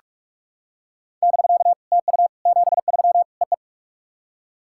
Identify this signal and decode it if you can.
Morse code, heard as a keyed tone, about 47 wpm, 700 Hz. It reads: <BK> TU 73 EE
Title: SOTA QSO Practice 45wpm